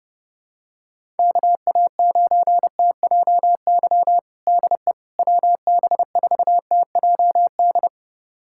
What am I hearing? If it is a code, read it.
Morse code, heard as a keyed tone, about 30 wpm, 700 Hz. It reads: KA9TJY DE WB4TJD
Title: USA Callsign DE Callsign 30wpm